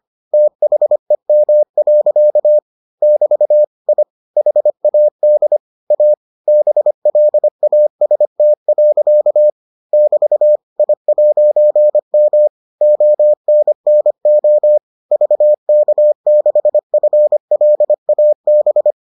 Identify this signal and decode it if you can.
Morse code, heard as a keyed tone, about 25 wpm, 600 Hz. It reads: THEM. <BT> I HAD A BLAST. <BT> I'M ONNO VK6FLAB